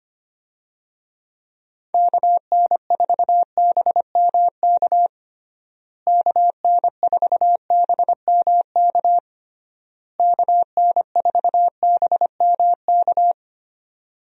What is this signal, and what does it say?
Morse code, heard as a keyed tone, about 25 wpm, 700 Hz. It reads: KN4BMK KN4BMK KN4BMK
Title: USA Call Signs - 6 Characters - Repeated 3x - 25wpm